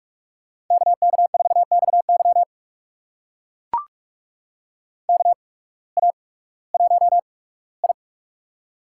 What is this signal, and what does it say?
Morse code, heard as a keyed tone, about 44 wpm, spaced as for 21 wpm, 700 Hz. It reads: KK4XY KA1I
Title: USA Call Signs - 8x Character Spacing - 45wpm